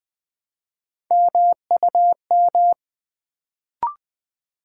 Morse code, heard as a keyed tone, about 20 wpm, 700 Hz. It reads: MUM
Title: Build A Word - 1 Letter At A Time - 3-Letter Words - 20wpm